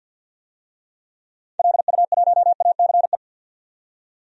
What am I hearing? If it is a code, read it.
Morse code, heard as a keyed tone, about 47 wpm, 700 Hz. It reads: LU1ACE
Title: DXCC-100 Call Signs - 6 Characters - 50wpm